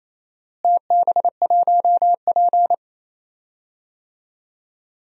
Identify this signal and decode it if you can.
Morse code, heard as a keyed tone, about 27 wpm, 700 Hz. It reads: TB1P
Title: DXCC-100 Call Signs - 4 Characters - Repeated 3x - 28wpm